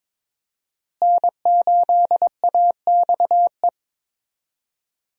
Morse code, heard as a keyed tone, about 22 wpm, 700 Hz. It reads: N8AXE
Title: USA Call Signs - 5 Characters - Repeated 3x - 22wpm